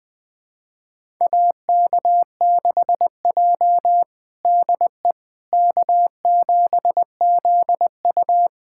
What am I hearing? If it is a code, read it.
Morse code, heard as a keyed tone, about 20 wpm, 700 Hz. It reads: AK6J DE K7ZU
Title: USA Callsign DE Callsign - 4 Characters - 20wpm